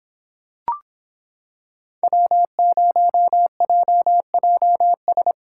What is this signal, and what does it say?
Morse code, heard as a keyed tone, about 23 wpm, 700 Hz. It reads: W0JJS